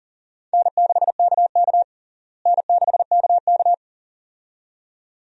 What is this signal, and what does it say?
Morse code, heard as a keyed tone, about 37 wpm, 700 Hz. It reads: N6KK N6KK
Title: USA Call Signs - 4 Characters - Repeated 2x - 40wpm